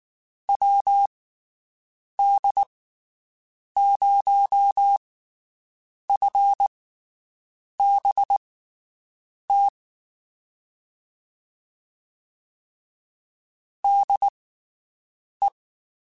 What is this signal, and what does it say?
Morse code, heard as a keyed tone, about 19 wpm, spaced as for 7 wpm, 800 Hz. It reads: WD0FBT DE
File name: QSOs Spaced